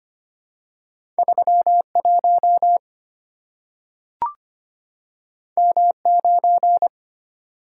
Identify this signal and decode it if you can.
Morse code, heard as a keyed tone, about 25 wpm, 690 Hz. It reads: 31 M9
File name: Two Letter-Number Combinations - 25wpm